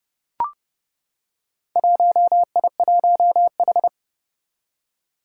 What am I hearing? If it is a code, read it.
Morse code, heard as a keyed tone, about 30 wpm, 700 Hz. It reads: 1I1H
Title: Four Challenging Letter-Number Combinations 30wpm